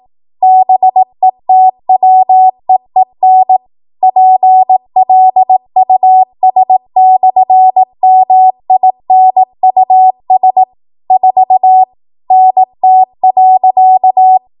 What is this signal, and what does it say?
Morse code, heard as a keyed tone, about 18 wpm, 750 Hz. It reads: BETWEEN PLUS/MINUS 4 NT.
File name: Morse / 18WPM